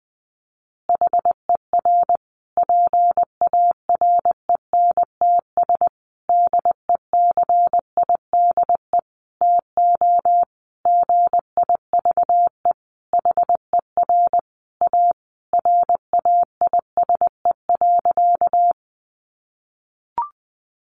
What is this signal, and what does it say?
Morse code, heard as a keyed tone, about 20 wpm, 700 Hz. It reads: HER PARENTS DECIDE TO GIVE HER A RAISE.
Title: Easy Phrases 20wpm